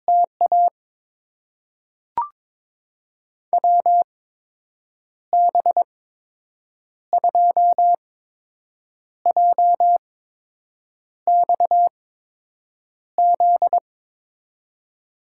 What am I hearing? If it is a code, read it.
Morse code, heard as a keyed tone, about 22 wpm, spaced as for 7 wpm, 700 Hz. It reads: TA WB2JXZ
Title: USA Call Signs - 8x Character Spacing - 22wpm